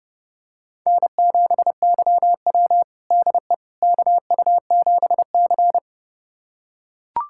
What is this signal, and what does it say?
Morse code, heard as a keyed tone, about 30 wpm, 700 Hz. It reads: N7YW DE KU7C
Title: USA Callsign DE Callsign - 4 Characters - 30wpm